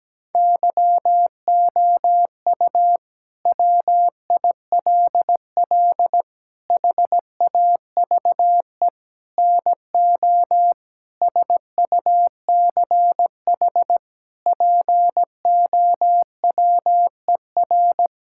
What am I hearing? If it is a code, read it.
Morse code, heard as a keyed tone, about 17 wpm, 700 Hz. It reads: YOU WILL HAVE NO SUCH POWER
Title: Sentences from Top 1000 Words - Review Part 6 - 17wpm